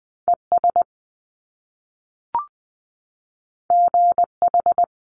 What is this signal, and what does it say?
Morse code, heard as a keyed tone, about 20 wpm, 700 Hz. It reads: ES GH
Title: Top Two Letter Combinations in Words 20wpm